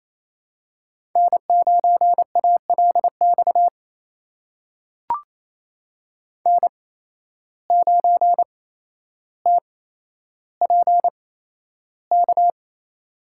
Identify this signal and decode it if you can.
Morse code, heard as a keyed tone, about 27 wpm, spaced as for 12 wpm, 700 Hz. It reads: N9ALX N9TPK
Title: USA Call Signs - 5 Characters - 8x Character Spacing - 28wpm